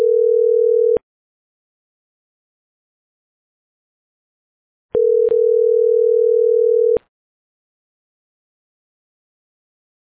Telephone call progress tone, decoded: ringback tone